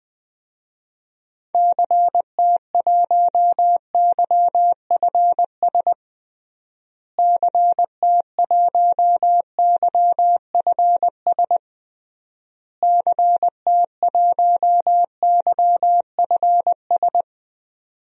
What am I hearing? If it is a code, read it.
Morse code, heard as a keyed tone, about 20 wpm, 700 Hz. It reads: CT1YFS CT1YFS CT1YFS
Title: DXCC-100 Call Signs - 6 Characters - Repeated 3x - 20wpm